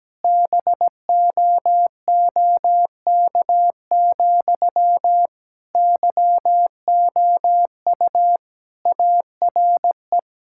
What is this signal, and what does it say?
Morse code, heard as a keyed tone, about 17 wpm, 700 Hz. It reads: BOOK, YOU ARE
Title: Sentences from Top 1000 Words - Review Part 5 - 17wpm